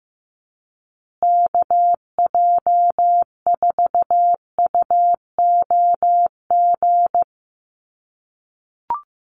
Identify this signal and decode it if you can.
Morse code, heard as a keyed tone, about 15 wpm, 700 Hz. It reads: KJ4UOG
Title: USA Call Signs Repeated 3x 15wpm